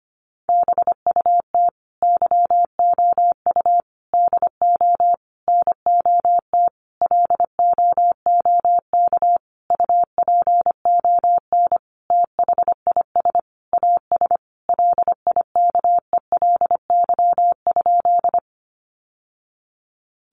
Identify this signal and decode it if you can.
Morse code, heard as a keyed tone, about 25 wpm, 700 Hz. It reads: BUT YOU DO NOT LOOK UPON THIS AS LIKELY?
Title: Sentences from Top 1100 Words 25wpm